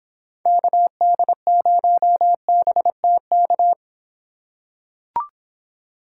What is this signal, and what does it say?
Morse code, heard as a keyed tone, about 25 wpm, 700 Hz. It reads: KD0BTK